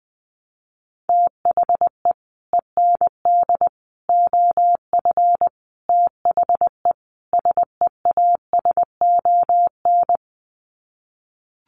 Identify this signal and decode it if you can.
Morse code, heard as a keyed tone, about 20 wpm, 700 Hz. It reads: THE END OF THE SEASON
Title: Sets of 5 Words Encore 20wpm